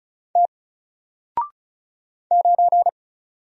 Morse code, heard as a keyed tone, about 36 wpm, 700 Hz. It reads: T 9
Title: Single Letter-Number Alt 35wpm